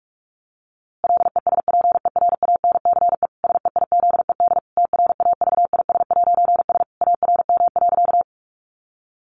Morse code, heard as a keyed tone, about 44 wpm, 700 Hz. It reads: LESPERANCE SEIZED TRAVIS'S ARM.
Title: A Sound Of Thunder 45wpm